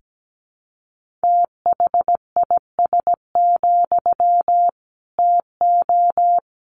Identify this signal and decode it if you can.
Morse code, heard as a keyed tone, about 17 wpm, 700 Hz. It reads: THIS, TO
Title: Sentences from Top 1500 Words 17wpm